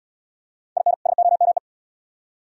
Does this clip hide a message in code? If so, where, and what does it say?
Yes: Morse code from 0.8 to 1.6 s, heard as a keyed tone, about 68 wpm, 700 Hz. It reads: IN SOME